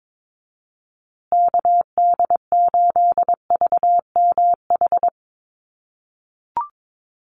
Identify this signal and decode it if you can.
Morse code, heard as a keyed tone, about 22 wpm, 700 Hz. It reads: KD8VMH
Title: USA Call Signs Repeated 3x 22wpm